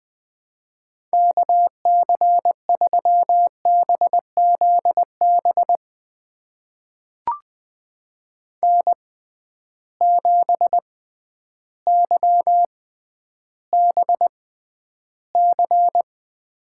Morse code, heard as a keyed tone, about 20 wpm, spaced as for 11 wpm, 700 Hz. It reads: KC3BZB N7YBC
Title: USA Call Signs - 6x Character Spacing - 20wpm